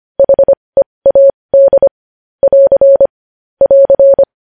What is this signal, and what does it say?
Morse code, heard as a keyed tone, about 25 wpm, spaced as for 20 wpm, 550 Hz. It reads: HEAD <AR> <AR>
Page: news-headlines-in-morse-code-at-20-wpm